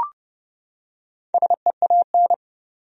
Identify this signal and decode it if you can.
Morse code, heard as a keyed tone, about 32 wpm, 700 Hz. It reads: SEAN
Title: USA Common Names - 30wpm